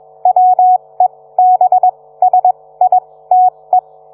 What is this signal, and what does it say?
Morse code, heard as a keyed tone, about 21 wpm, spaced as for 17 wpm, 740 Hz. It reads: WEBSITE